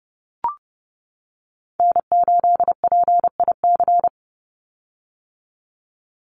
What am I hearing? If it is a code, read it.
Morse code, heard as a keyed tone, about 30 wpm, 700 Hz. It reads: N8PIC
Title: USA Call Signs 30wpm